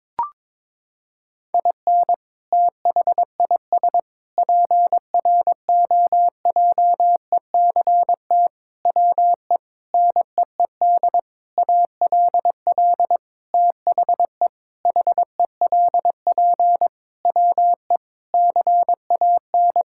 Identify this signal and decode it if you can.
Morse code, heard as a keyed tone, about 22 wpm, 700 Hz. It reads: IN THIS PROJECT WE NEED ALL THE HELP WE CAN
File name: practice / 22wpm / Sentences with 8-Word Phrases - 22wpm